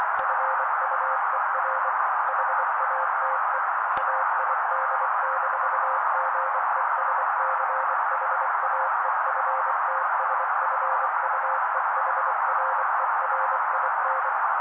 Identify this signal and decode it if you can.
Morse code, heard as a keyed tone, about 24 wpm, 530 Hz. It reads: FUER HATE AID-GESCHAEFTSFUEHRERIN
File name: daily / 23wpm noise